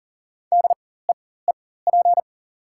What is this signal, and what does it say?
Morse code, heard as a keyed tone, about 40 wpm, spaced as for 19 wpm, 700 Hz. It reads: DEEP
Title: 4x Character Spacing - Top 500 Words - 40wpm